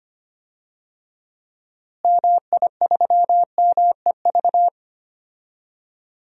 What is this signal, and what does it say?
Morse code, heard as a keyed tone, about 25 wpm, 700 Hz. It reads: MI3MEV
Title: DXCC-100 Call Signs - 6 Characters - Repeated 2x - 25wpm